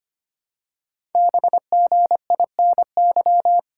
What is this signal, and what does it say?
Morse code, heard as a keyed tone, about 25 wpm, 700 Hz. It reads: BGINY
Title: Code Groups - 5 Letters - 25wpm